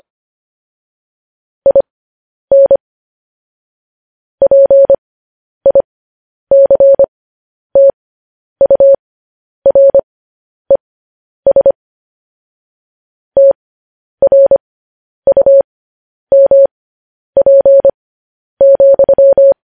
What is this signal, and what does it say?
Morse code, heard as a keyed tone, about 25 wpm, spaced as for 10 wpm, 550 Hz. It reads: IN PICTURES TRUMP,
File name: podcast / News headlines-1x-25x10